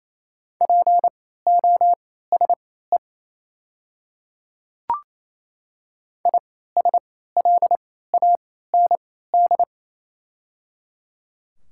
Morse code, heard as a keyed tone, about 27 wpm, spaced as for 16 wpm, 690 Hz. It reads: POSE ISLAND
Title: 3x Character Spacing - Top 500 Words - 28wpm